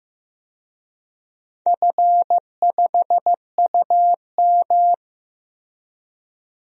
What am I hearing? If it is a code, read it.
Morse code, heard as a keyed tone, about 15 wpm, 700 Hz. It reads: F5UM
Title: DXCC-100 Call Signs - 4 Characters - Repeated 2x - 15wpm